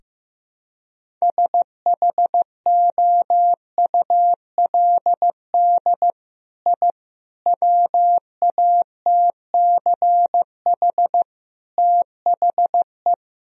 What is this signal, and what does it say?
Morse code, heard as a keyed tone, about 15 wpm, 700 Hz. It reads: SHOULD I WATCH THE
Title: Sentences from Top 1000 Words - Review Part 4 - 15wpm